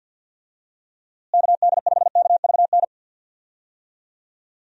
Morse code, heard as a keyed tone, about 52 wpm, 700 Hz. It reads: KD5KVN
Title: USA Call Signs - 6 Characters - Repeated 3x - 50wpm